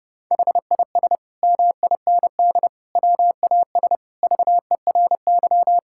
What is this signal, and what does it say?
Morse code, heard as a keyed tone, about 32 wpm, 700 Hz. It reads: HIS MIND WAS VERY
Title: Sentences from Top 1000 Words - Review Part 6 - 30wpm